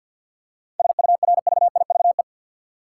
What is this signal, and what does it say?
Morse code, heard as a keyed tone, about 53 wpm, 700 Hz. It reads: SURVIVE
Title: Seven Letter Words - 50wpm